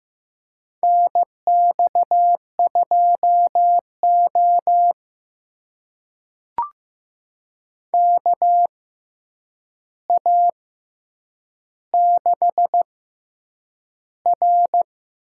Morse code, heard as a keyed tone, about 15 wpm, spaced as for 8 wpm, 700 Hz. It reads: NX2O KA6R
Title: USA Call Signs - 4 Characters - 6x Character Spacing - 15wpm